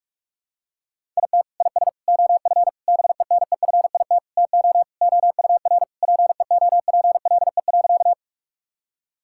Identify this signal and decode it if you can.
Morse code, heard as a keyed tone, about 41 wpm, 700 Hz. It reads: IT IS OF BENEFIT TO OUR PEOPLE.
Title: Sentences from Top 1000 Words - Review Part 6 - 45wpm